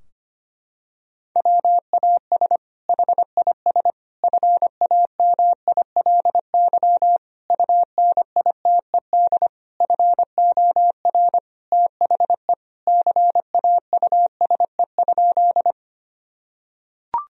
Morse code, heard as a keyed tone, about 25 wpm, 700 Hz. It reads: WAS HIS FAMILY UNITED FOR THE CAUSE?
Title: Sentences from Top 1000 Words - Review Part 4 - 25wpm